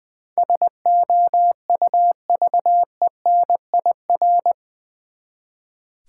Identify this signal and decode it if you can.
Morse code, heard as a keyed tone, about 20 wpm, 690 Hz. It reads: SOUVENIR
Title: Easy Words with Hints - 20wpm